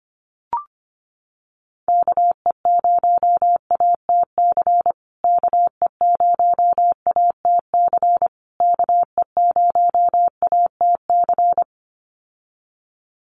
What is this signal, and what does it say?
Morse code, heard as a keyed tone, about 25 wpm, 700 Hz. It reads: KE0ATC KE0ATC KE0ATC
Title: USA Call Signs Repeated 3x 25wpm